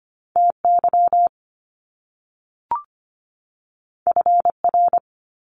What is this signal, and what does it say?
Morse code, heard as a keyed tone, about 25 wpm, 700 Hz. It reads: TY FR
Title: Top Two Letter Combinations in Words 25wpm